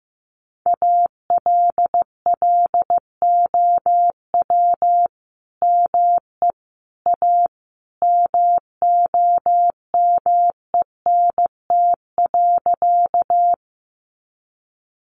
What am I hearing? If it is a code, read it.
Morse code, heard as a keyed tone, about 15 wpm, 700 Hz. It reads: ALLOW ME A MOMENT.
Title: Sentences from Top 1000 Words 15wpm